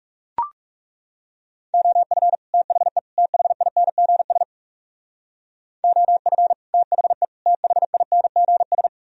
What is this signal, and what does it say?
Morse code, heard as a keyed tone, about 35 wpm, 700 Hz. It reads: OF THE THINGS OF THE THINGS
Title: Speed-Racing - Sets of 3 Words 30wpm